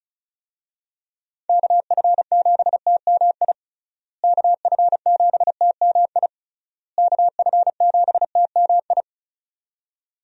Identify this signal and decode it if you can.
Morse code, heard as a keyed tone, about 34 wpm, 700 Hz. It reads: KF7TMI KF7TMI KF7TMI
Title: USA Call Signs - 6 Characters - Repeated 3x - 35wpm